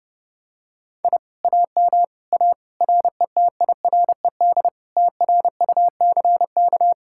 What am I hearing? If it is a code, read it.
Morse code, heard as a keyed tone, about 30 wpm, 700 Hz. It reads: I AM A RETIRED TRUCK